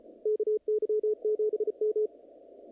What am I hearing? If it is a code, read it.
Morse code, heard as a keyed tone, about 35 wpm, 430 Hz, with radio noise. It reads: KY7M
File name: contests / naqp-cw-k5tr-op-k5pi- -2200